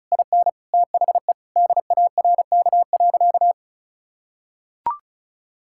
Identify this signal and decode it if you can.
Morse code, heard as a keyed tone, about 36 wpm, 700 Hz. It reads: IN THE DARK.
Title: Sentences from Top 1600 Words 35wpm